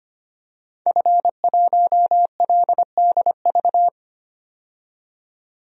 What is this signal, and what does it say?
Morse code, heard as a keyed tone, about 25 wpm, 700 Hz. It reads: F1LDV
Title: DXCC-100 Call Signs - 5 Characters - 25wpm